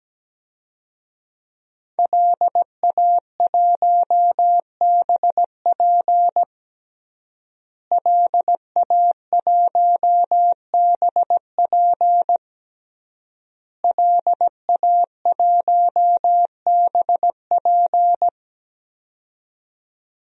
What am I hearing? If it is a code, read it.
Morse code, heard as a keyed tone, about 17 wpm, 700 Hz. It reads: LA1BP LA1BP LA1BP
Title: DXCC-100 Call Signs - 5 Characters - Repeated 3x - 17wpm